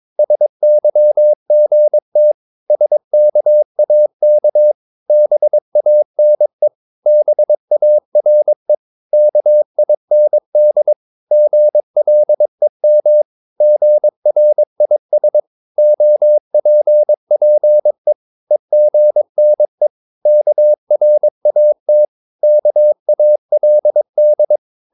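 Morse code, heard as a keyed tone, about 23 wpm, 600 Hz. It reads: SYGT SKAK BANE BARE KIND GLEM GRIS OPPE EGNE KRAT KALD
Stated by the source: Morse practice of short Danish words at 22 wpm, from CW med Gnister.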